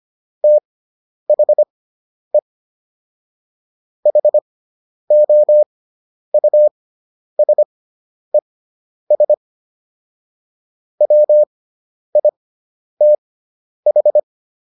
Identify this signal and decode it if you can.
Morse code, heard as a keyed tone, about 25 wpm, spaced as for 10 wpm, 600 Hz. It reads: THE HOUSES WITH